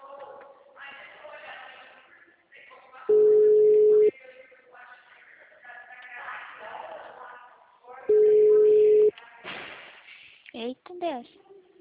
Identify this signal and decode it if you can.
Ringback tone (Europe, 425 Hz), with speech.